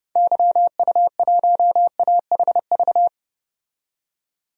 Morse code, heard as a keyed tone, about 30 wpm, 700 Hz. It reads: YU1AHV
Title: DXCC-100 Call Signs - 6 Characters - 30wpm